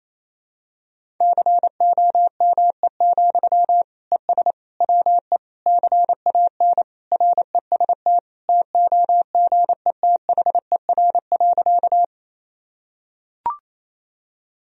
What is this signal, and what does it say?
Morse code, heard as a keyed tone, about 28 wpm, 700 Hz. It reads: COME, ES WE CAN REST TOGETHER.
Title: Sentences from Top 500 Words and Abbreviations 28wpm